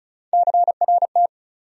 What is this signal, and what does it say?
Morse code, heard as a keyed tone, about 36 wpm, 700 Hz. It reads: CRT
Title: USA Call Sign Postfixes - 3 Letters 35wpm